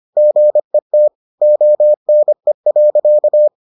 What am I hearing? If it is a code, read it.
Morse code, heard as a keyed tone, about 25 wpm, 600 Hz. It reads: GET ONE.